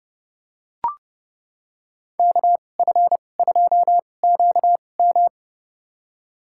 Morse code, heard as a keyed tone, about 30 wpm, spaced as for 22 wpm, 700 Hz. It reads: KF2QM
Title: USA Call Signs - 5 Characters - 2x Character Spacing - 30wpm